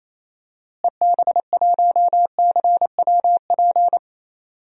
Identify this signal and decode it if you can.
Morse code, heard as a keyed tone, about 28 wpm, 700 Hz. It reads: EB1CWP
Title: DXCC-100 Call Signs - 6 Characters - Repeated 3x - 28wpm